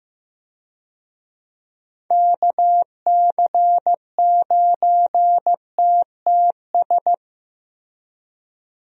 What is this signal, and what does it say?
Morse code, heard as a keyed tone, about 15 wpm, 700 Hz. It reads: KC9TTS
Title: USA Call Signs - 6 Characters - Repeated 3x - 15wpm